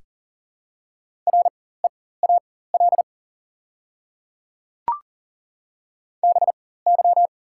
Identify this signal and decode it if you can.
Morse code, heard as a keyed tone, about 40 wpm, spaced as for 19 wpm, 700 Hz. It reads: REAL BY
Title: 4x Character Spacing - Top 300 Words - 40wpm